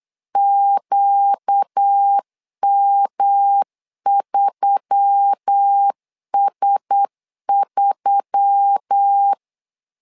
Morse code, heard as a keyed tone, about 8 wpm, 800 Hz. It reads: QM3S3
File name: LEZIONI / 4 Q1S3 8wpm 3037